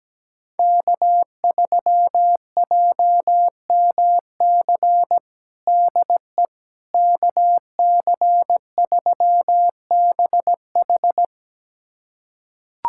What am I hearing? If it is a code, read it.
Morse code, heard as a keyed tone, about 17 wpm, 700 Hz. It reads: K3JMC DE KC3BH
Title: USA Callsign DE Callsign - 5 Characters - 17wpm